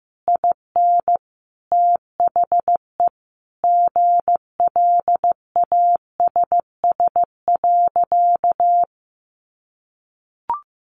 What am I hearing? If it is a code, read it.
Morse code, heard as a keyed tone, about 15 wpm, 700 Hz. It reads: IN THE GLASS.
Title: Sentences from Top 1300 Words 15wpm